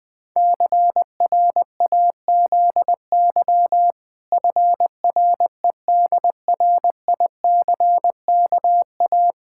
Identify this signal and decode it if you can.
Morse code, heard as a keyed tone, about 20 wpm, 700 Hz. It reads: CRAZY FREDRICKA
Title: Pangram Challenge v2 - 20wpm